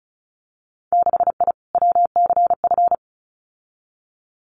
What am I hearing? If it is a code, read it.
Morse code, heard as a keyed tone, about 36 wpm, 700 Hz. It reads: 6I WCF
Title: Winter Field Day Exchanges Encore 35wpm